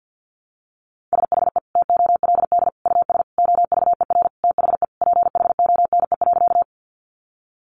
Morse code, heard as a keyed tone, about 48 wpm, 700 Hz. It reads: SHE TOLD US OVER THE PHONE.
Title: Sentences from Top 1100 Words 50wpm